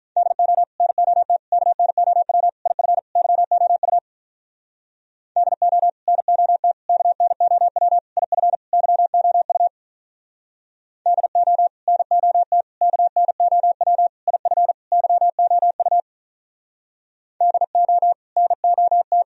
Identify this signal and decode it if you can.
Morse code, heard as a keyed tone, about 46 wpm, 700 Hz. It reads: DO NOT KNOW IF YOU DO NOT KNOW IF YOU DO NOT KNOW IF YOU DO NOT